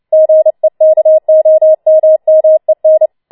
Morse code, heard as a keyed tone, about 29 wpm, 630 Hz. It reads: GEKOMMEN
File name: Schneewittchen Teil3 30wpm